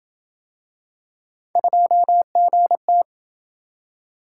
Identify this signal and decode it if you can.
Morse code, heard as a keyed tone, about 27 wpm, 700 Hz. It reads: 2GT